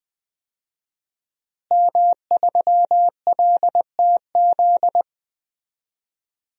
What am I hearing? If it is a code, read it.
Morse code, heard as a keyed tone, about 21 wpm, 700 Hz. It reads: M3LTZ
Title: DXCC-100 Call Signs - 5 Characters - Repeated 3x - 20wpm